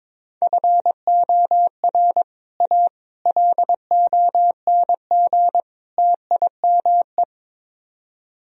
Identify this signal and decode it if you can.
Morse code, heard as a keyed tone, about 22 wpm, 700 Hz. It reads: FOR A LONG TIME
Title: Speed-Racing - Sets of 4 Words 22wpm